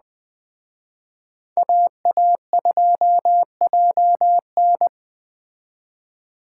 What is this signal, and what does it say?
Morse code, heard as a keyed tone, about 20 wpm, 700 Hz. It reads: AA2JN
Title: Speed Racing - USA Call Signs 17wpm